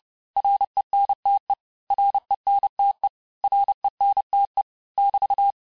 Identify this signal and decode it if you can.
Morse code, heard as a keyed tone, about 30 wpm, 800 Hz. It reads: RENTE RENTE RENTE <BT>
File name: cours-du-mercredi / audios / cours-854-mots-vitesse-28-mpm